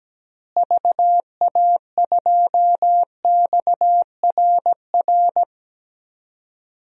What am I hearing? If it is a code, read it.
Morse code, heard as a keyed tone, about 17 wpm, 700 Hz. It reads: VA2XRR
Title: DXCC-100 Call Signs - 6 Characters - Repeated 3x - 17wpm